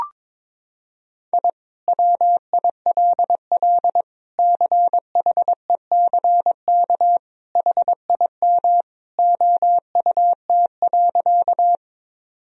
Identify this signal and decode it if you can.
Morse code, heard as a keyed tone, about 22 wpm, 700 Hz. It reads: I WILL CHECK HIM OUT.
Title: Sentences from Top 1000 Words - Review Part 3 - 22wpm